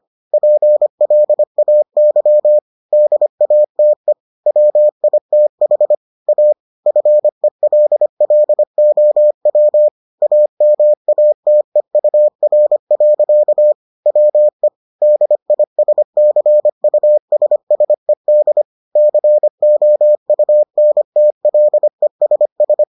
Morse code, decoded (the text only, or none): PLAY DATE WITH A FELLOW AMATEUR. WE DISCUSSED COUNTLESS